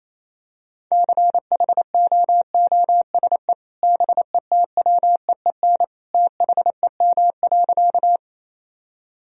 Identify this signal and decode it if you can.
Morse code, heard as a keyed tone, about 28 wpm, 700 Hz. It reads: CHOOSE BETWEEN THEM.
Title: Sentences from Top 1000 Words - Review Part 4 - 28wpm